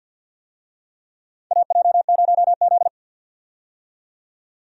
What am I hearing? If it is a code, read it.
Morse code, heard as a keyed tone, about 46 wpm, 700 Hz. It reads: AJ0Z
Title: USA Call Signs - 4 Characters - 50wpm